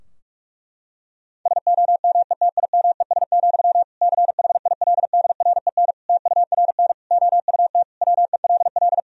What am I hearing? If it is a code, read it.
Morse code, heard as a keyed tone, about 41 wpm, 700 Hz. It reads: SOMETIMES, CHILDREN TURN OUT WELL
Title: Sentences from Top 1000 Words - Review Part 3 - 45wpm